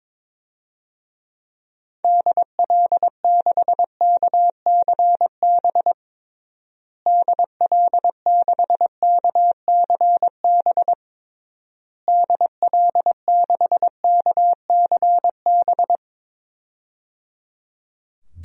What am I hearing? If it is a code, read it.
Morse code, heard as a keyed tone, about 22 wpm, 700 Hz. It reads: DL6KCB DL6KCB DL6KCB
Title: DXCC-100 Call Signs - 6 Characters - Repeated 3x - 22wpm